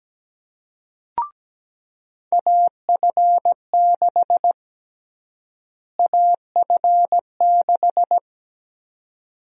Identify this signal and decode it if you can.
Morse code, heard as a keyed tone, about 17 wpm, 700 Hz. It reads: AF6 AF6
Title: USA Call Sign Prefixes - Repeated 2x - 17wpm